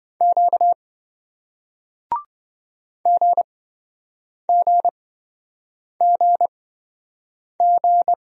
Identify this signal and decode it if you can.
Morse code, heard as a keyed tone, about 27 wpm, spaced as for 7 wpm, 700 Hz. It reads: Q GGGG